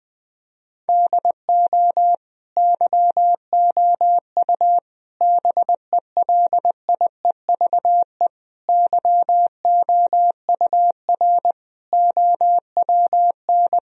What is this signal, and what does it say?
Morse code, heard as a keyed tone, about 20 wpm, 700 Hz. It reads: DO YOU BELIEVE YOUR OWN
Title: Sentences from Top 1000 Words - Review Part 3 - 20wpm